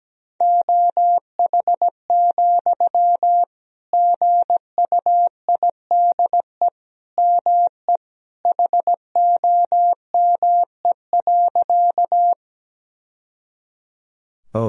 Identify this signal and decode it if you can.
Morse code, heard as a keyed tone, about 17 wpm, 700 Hz. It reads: OH, GUIDE ME HOME.
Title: Sentences from Top 1000 Words - Review Part 3 - 17wpm